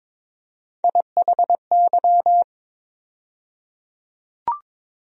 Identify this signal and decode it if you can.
Morse code, heard as a keyed tone, about 22 wpm, 700 Hz. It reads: IHY
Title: USA Call Sign Postfixes - 3 Letters 22wpm